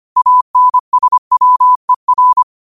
Morse code, heard as a keyed tone, about 25 wpm, 1000 Hz. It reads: ANSWER
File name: qotdfiles / 25wpm